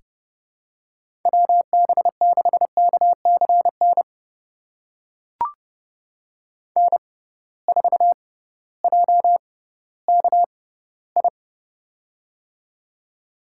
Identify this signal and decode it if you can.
Morse code, heard as a keyed tone, about 30 wpm, spaced as for 16 wpm, 700 Hz. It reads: WB6KCN N4JKI
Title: USA Call Signs - 6x Character Spacing - 30wpm